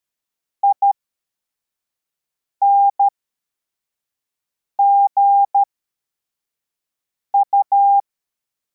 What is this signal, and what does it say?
Morse code, heard as a keyed tone, about 13 wpm, spaced as for 4 wpm, 800 Hz. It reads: INGU
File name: formation / cw / Text44